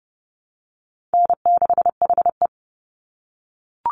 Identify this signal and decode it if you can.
Morse code, heard as a keyed tone, about 30 wpm, 700 Hz. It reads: N6HE
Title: Easy Call Signs - 30wpm